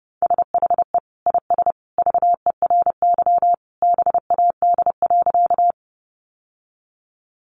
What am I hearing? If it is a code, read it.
Morse code, heard as a keyed tone, about 30 wpm, 700 Hz. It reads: SHE IS VERY BAD.
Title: Sentences from Top 700 Words 30wpm